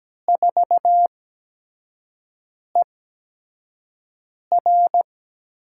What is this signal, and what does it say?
Morse code, heard as a keyed tone, about 17 wpm, spaced as for 5 wpm, 700 Hz. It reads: 4ER
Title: USA Call Signs - 4 Characters - 8x Character Spacing - 17wpm